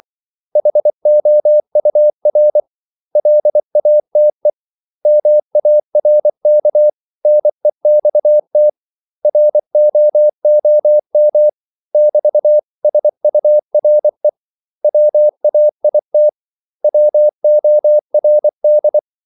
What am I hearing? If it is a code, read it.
Morse code, heard as a keyed tone, about 24 wpm, 600 Hz. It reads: HOUR LATE MARK NEXT ROOM <BT>SURE WAIT WORD
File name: CW / Practice / 500MostCommonEnglishWords24